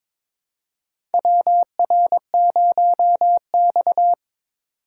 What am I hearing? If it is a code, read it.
Morse code, heard as a keyed tone, about 21 wpm, 700 Hz. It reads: WR0X
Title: USA Call Signs - 4 Characters - 2x Character Spacing - 22wpm